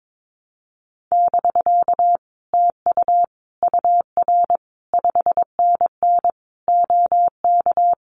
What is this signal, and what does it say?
Morse code, heard as a keyed tone, about 22 wpm, 700 Hz. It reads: <BK> TU UR 5NN OK
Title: Common QSO Phrases Encore 22wpm